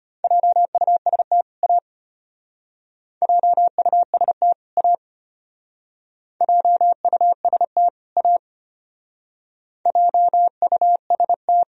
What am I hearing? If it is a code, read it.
Morse code, heard as a keyed tone, about 34 wpm, 700 Hz. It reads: JUST A JUST A JUST A JUST